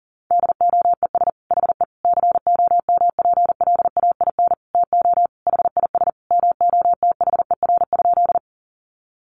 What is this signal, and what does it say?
Morse code, heard as a keyed tone, about 40 wpm, 700 Hz. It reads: DOES HE COMPLAIN TO HIS MOTHER?
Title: Sentences from Top 1300 Words 40wpm